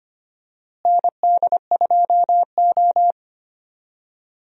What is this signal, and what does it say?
Morse code, heard as a keyed tone, about 25 wpm, 700 Hz. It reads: ND2O
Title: USA Call Signs - 4 Characters - Repeated 2x - 25wpm